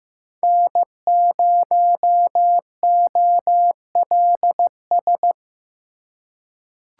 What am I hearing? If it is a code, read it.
Morse code, heard as a keyed tone, about 15 wpm, 700 Hz. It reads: N0OLS
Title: USA Call Signs - 5 Characters - 3x Character Spacing - 15wpm